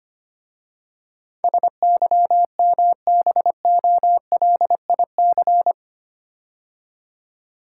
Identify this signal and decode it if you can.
Morse code, heard as a keyed tone, about 25 wpm, 700 Hz. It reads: SYMBOLIC
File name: 8 Letter Words - First 3 Letters Given - 25wpm